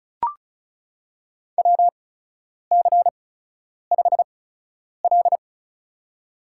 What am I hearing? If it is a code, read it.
Morse code, heard as a keyed tone, about 34 wpm, spaced as for 10 wpm, 700 Hz. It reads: WC5L
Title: USA Call Signs - 4 Characters - 8x Character Spacing - 35wpm